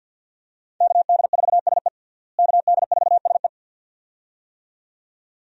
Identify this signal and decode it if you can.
Morse code, heard as a keyed tone, about 45 wpm, 700 Hz. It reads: KD4SE KD4SE
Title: USA Call Signs - 5 Characters - Repeated 2x - 50wpm